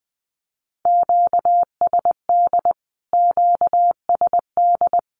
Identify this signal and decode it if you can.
Morse code, heard as a keyed tone, about 20 wpm, 700 Hz. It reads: QSD QSD
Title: Q-Codes Repeated 3x 20wpm